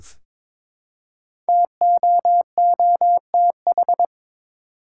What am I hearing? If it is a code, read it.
Morse code, heard as a keyed tone, about 22 wpm, 700 Hz. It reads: TOOTH
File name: 5 Letter Words - First 3 Letters Given - 22wpm